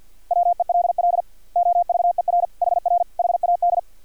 Morse code, heard as a keyed tone, about 50 wpm, 700 Hz. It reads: WELL OVER HR HAD
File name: archived / CRSnet